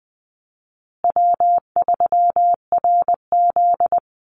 Morse code, heard as a keyed tone, about 19 wpm, 700 Hz. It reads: W3RZ
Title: Easy Call Signs Repeated 3x - 20wpm